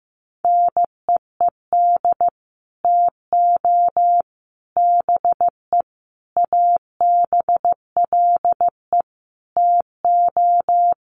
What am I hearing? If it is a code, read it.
Morse code, heard as a keyed tone, about 15 wpm, 700 Hz. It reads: NEED TO BE ABLE TO
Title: Sets of 5 Words Encore 15wpm